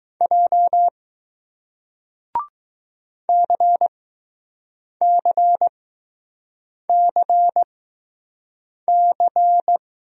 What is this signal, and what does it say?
Morse code, heard as a keyed tone, about 20 wpm, spaced as for 6 wpm, 700 Hz. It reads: J CCCC